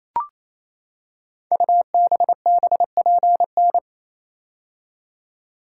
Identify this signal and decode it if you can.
Morse code, heard as a keyed tone, about 29 wpm, 700 Hz. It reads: UBBPN
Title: Code Groups - 5 Letter:Numbers - 28wpm